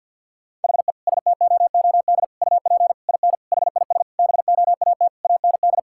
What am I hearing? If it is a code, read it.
Morse code, heard as a keyed tone, about 50 wpm, 700 Hz. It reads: HE STOOD UP IN HIS BOAT AND
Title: Sentences from Top 1600 Words 50wpm